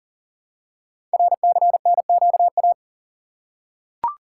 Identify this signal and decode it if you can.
Morse code, heard as a keyed tone, about 42 wpm, 700 Hz. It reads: RCNQA
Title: Code Groups - 5 Letters - 40wpm